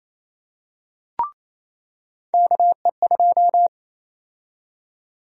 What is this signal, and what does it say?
Morse code, heard as a keyed tone, about 28 wpm, 700 Hz. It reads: KE2
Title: USA Call Sign Prefixes - 28wpm